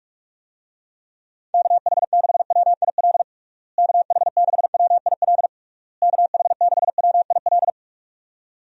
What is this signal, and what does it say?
Morse code, heard as a keyed tone, about 46 wpm, 700 Hz. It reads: KH6WIL KH6WIL KH6WIL
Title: USA Call Signs - 6 Characters - Repeated 3x - 45wpm